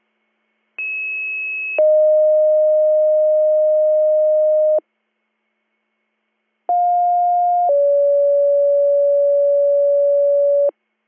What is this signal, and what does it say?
Two-tone page, heard as a one-second tone, then a three-second tone, about 2600 then 620 Hz, about 710 then 570 Hz. It is heard from 0.8 to 10.7 s.